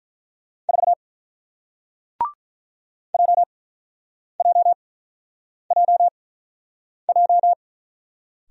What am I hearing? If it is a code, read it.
Morse code, heard as a keyed tone, about 44 wpm, spaced as for 8 wpm, 700 Hz. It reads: 4 JJJJ
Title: Speed-Racing Single Letter-Number 35wpm